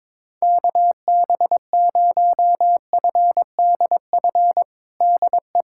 Morse code, heard as a keyed tone, about 22 wpm, 700 Hz. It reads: KB0FDF DE
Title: USA Callsign DE Callsign 22wpm